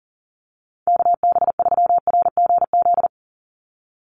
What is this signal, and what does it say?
Morse code, heard as a keyed tone, about 40 wpm, 700 Hz. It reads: KB3RGZ
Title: USA Call Signs 40wpm